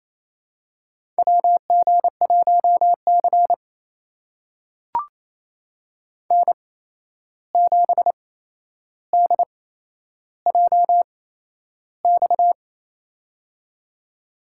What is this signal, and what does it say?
Morse code, heard as a keyed tone, about 28 wpm, spaced as for 11 wpm, 700 Hz. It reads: WG1C N7DJX
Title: USA Call Signs - 8x Character Spacing - 28wpm